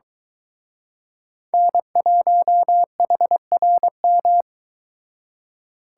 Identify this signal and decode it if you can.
Morse code, heard as a keyed tone, about 23 wpm, 700 Hz. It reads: N1HRM